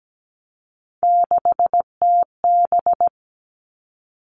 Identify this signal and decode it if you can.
Morse code, heard as a keyed tone, about 17 wpm, 700 Hz. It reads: 6TB
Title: All Cut Numbers - 3-Digit 17wpm